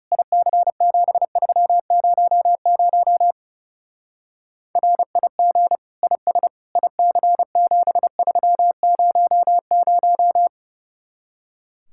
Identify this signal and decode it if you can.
Morse code, heard as a keyed tone, about 36 wpm, 700 Hz. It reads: IC7300 RIG IS IC7300
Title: Speed-Racing Common QSO Phrases - 30wpm